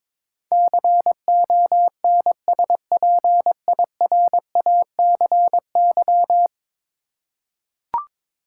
Morse code, heard as a keyed tone, about 21 wpm, 700 Hz. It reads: CONSPIRACY
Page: morse-code-ninja-practice-22wpm